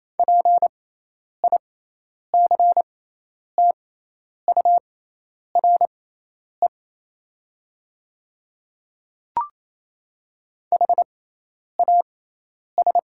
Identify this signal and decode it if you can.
Morse code, heard as a keyed tone, about 28 wpm, spaced as for 10 wpm, 700 Hz. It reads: PICTURE HAS
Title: 6x Character Spacing - Top 200 Words - 28wpm